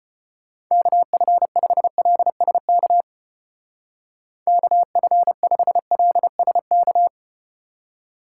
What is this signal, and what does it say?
Morse code, heard as a keyed tone, about 34 wpm, 700 Hz. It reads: KF5LSK KF5LSK